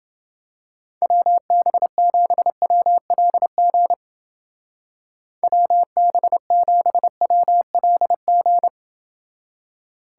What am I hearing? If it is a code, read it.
Morse code, heard as a keyed tone, about 29 wpm, 700 Hz. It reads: WB7WLG WB7WLG